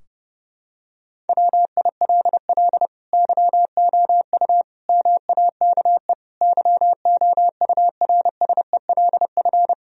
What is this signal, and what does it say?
Morse code, heard as a keyed tone, about 30 wpm, 700 Hz. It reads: WILL YOU MAKE YOURSELF
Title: Sentences from Top 1000 Words - Review Part 3 - 30wpm